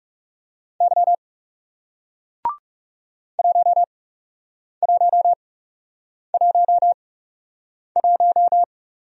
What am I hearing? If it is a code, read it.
Morse code, heard as a keyed tone, about 41 wpm, spaced as for 8 wpm, 700 Hz. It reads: Y 1111